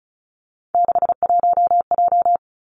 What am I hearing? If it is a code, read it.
Morse code, heard as a keyed tone, about 36 wpm, 700 Hz. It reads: 61J
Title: Three Challenging Letter-Number Combinations 35wpm